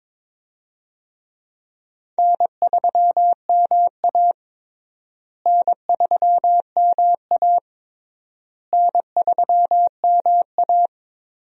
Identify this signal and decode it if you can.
Morse code, heard as a keyed tone, about 21 wpm, 700 Hz. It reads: N3MA N3MA N3MA
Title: USA Call Signs - 4 Characters - Repeated 3x - 22wpm